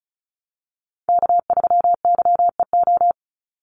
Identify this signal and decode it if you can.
Morse code, heard as a keyed tone, about 34 wpm, 700 Hz. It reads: K3YEO
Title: Call Signs - 35wpm